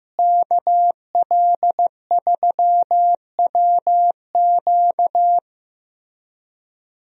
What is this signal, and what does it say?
Morse code, heard as a keyed tone, about 15 wpm, 700 Hz. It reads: KL3WQ
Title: USA Call Signs - 5 Characters - Repeated 3x - 15wpm